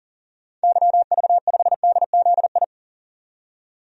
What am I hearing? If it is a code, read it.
Morse code, heard as a keyed tone, about 40 wpm, 700 Hz. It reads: YV5DZI
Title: DXCC-100 Call Signs - 6 Characters - 40wpm